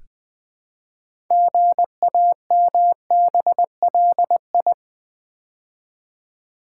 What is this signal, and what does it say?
Morse code, heard as a keyed tone, about 20 wpm, 700 Hz. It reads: GAMBLI
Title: Build A Word - 3 Letters At A Time - 8-Letter Words - 20wpm